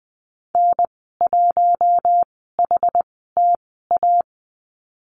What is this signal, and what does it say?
Morse code, heard as a keyed tone, about 20 wpm, spaced as for 14 wpm, 700 Hz. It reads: N1HTA
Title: 161 Lesson 26g - Letter g - Call Signs 20wpm